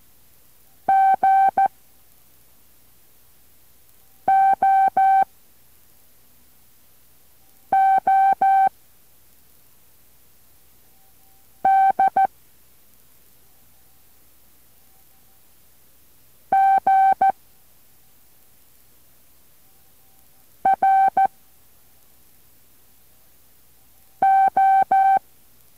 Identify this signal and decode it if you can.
Morse code, heard as a keyed tone, about 14 wpm, 780 Hz. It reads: G O O D G R O
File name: intl morse code2